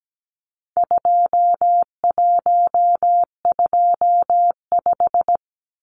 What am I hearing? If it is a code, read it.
Morse code, heard as a keyed tone, about 17 wpm, 700 Hz. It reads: 2125
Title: Numbers - 4-Digit 17wpm